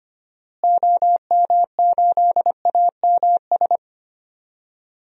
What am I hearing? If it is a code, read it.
Morse code, heard as a keyed tone, about 25 wpm, 700 Hz. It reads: OM8AMS
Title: DXCC-100 Call Signs - 6 Characters - Repeated 3x - 25wpm